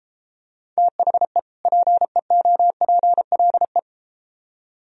Morse code, heard as a keyed tone, about 31 wpm, 700 Hz. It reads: THE PEOPLE